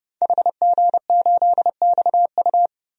Morse code, heard as a keyed tone, about 31 wpm, 700 Hz. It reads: HG8XU
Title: DXCC-100 Call Signs - 5 Characters - Repeated 3x - 30wpm